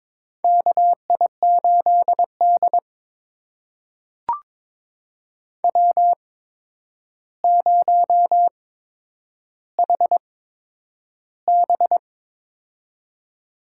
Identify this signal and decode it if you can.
Morse code, heard as a keyed tone, about 22 wpm, spaced as for 9 wpm, 700 Hz. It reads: KI8D W0HB
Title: USA Call Signs - 4 Characters - 8x Character Spacing - 22wpm